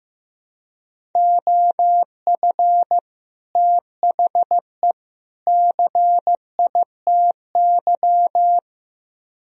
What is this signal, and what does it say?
Morse code, heard as a keyed tone, about 16 wpm, 700 Hz. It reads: OF THE CITY
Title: Speed-Racing - Sets of 3 Words 15wpm